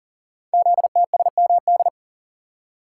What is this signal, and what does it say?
Morse code, heard as a keyed tone, about 40 wpm, 690 Hz. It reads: ZTSMD
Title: Code Groups - 5 Letters - 40wpm